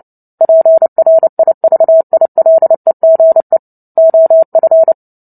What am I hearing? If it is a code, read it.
Morse code, heard as a keyed tone, about 30 wpm, 650 Hz. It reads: PRIVILEGE OF
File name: Book Two / 03-Chapter 3 - 29 WPM